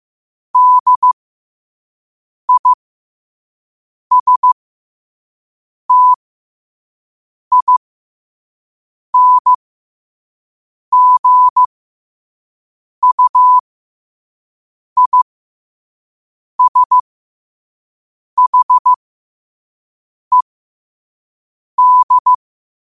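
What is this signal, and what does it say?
Morse code, heard as a keyed tone, about 15 wpm, spaced as for 5 wpm, 1000 Hz. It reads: DISTINGUISHED
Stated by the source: Morse at 5 wpm.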